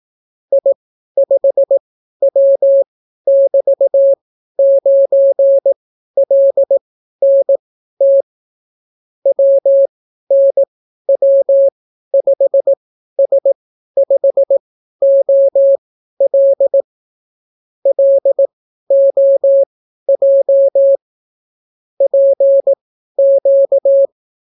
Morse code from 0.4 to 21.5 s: I5W<BT>9LNT WNW5S5OL LOJ